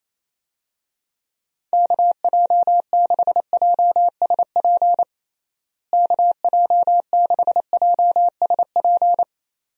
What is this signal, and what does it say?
Morse code, heard as a keyed tone, about 28 wpm, 700 Hz. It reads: KJ6JSP KJ6JSP
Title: USA Call Signs - 6 Characters - Repeated 2x - 28wpm